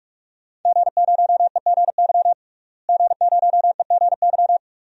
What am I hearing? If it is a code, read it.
Morse code, heard as a keyed tone, about 46 wpm, 700 Hz. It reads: G0EGY G0EGY
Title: DXCC-100 Call Signs - 5 Characters - Repeated 3x - 45wpm